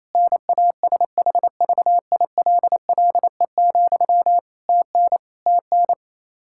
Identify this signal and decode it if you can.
Morse code, heard as a keyed tone, about 27 wpm, 700 Hz. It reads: NASHVILLE, TN TN
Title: Practice QSO - Exchange 1 - Beginning - 28wpm